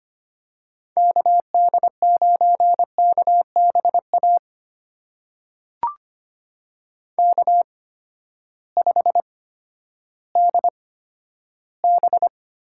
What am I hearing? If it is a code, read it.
Morse code, heard as a keyed tone, about 25 wpm, spaced as for 12 wpm, 700 Hz. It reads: KD9KBA K5DB
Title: USA Call Signs - 8x Character Spacing - 25wpm